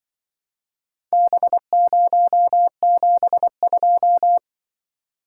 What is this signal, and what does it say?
Morse code, heard as a keyed tone, about 25 wpm, 700 Hz. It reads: B072
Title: Speed-Racing - Four Letter-Number Combinations - 20wpm